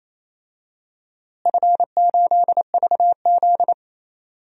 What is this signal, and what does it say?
Morse code, heard as a keyed tone, about 27 wpm, 700 Hz. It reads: F8VZ
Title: DXCC-100 Call Signs - 4 Characters - Repeated 2x - 28wpm